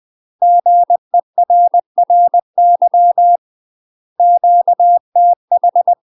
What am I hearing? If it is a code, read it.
Morse code, heard as a keyed tone, about 20 wpm, 700 Hz. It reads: GERRY QTH
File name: UK MASTER QSOs TRAINING / 20WPM / UK MASTER QSOs 0003